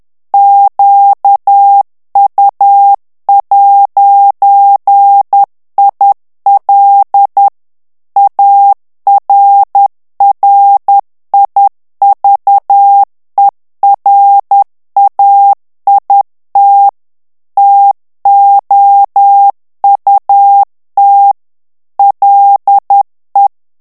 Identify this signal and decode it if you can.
Morse code, heard as a keyed tone, about 10 wpm, 800 Hz. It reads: QU'IL ARRIVERAIT TOUT LE
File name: CW / FICHIERS-MP3 / FRANCAIS TEXTE CLAIR 2 VITESSE 10 MOTS MINUTES